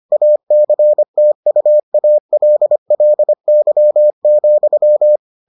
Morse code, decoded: ACTUALLY,

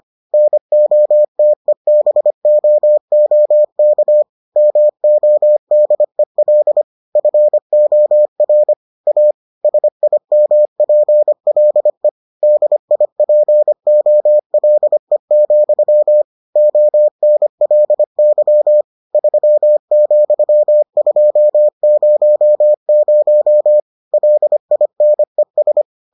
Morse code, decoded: NOTEBOOK MODEL FOR A SIMPLE DIPOLE, ONLY 3,200 LINES

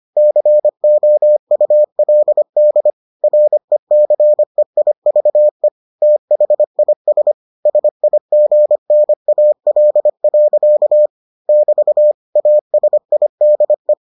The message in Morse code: COULD RECEIVE THIS SIGNAL. <BT> ASIDE

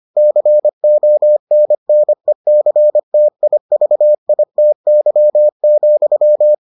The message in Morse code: CONNECTIVITY,

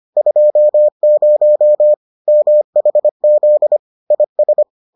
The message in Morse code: 20 MHZ IS

